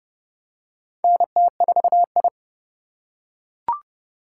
Morse code, heard as a keyed tone, about 30 wpm, 700 Hz. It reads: NT4I
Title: USA Call Signs - 4 Characters - 4x Character Spacing - 30wpm